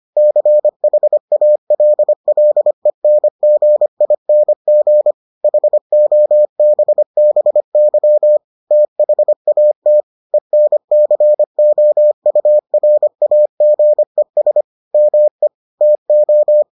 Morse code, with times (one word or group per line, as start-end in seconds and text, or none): 0.2-5.1 CHALLENGING
5.4-8.4 HOBBY
8.7-10.0 THAT
10.3-14.6 ENCOURAGES
14.9-15.5 ME
15.8-16.6 TO